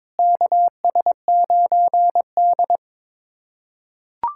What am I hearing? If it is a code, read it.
Morse code, heard as a keyed tone, about 22 wpm, 700 Hz. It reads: KS9D
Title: USA Callsign DE Callsign - 4 Characters - 22wpm